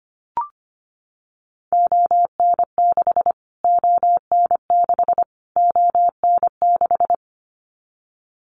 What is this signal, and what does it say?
Morse code, heard as a keyed tone, about 25 wpm, 700 Hz. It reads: ON6 ON6 ON6
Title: Callsign Prefixes for Top 25 DXCC - Repeated 3x - 25wpm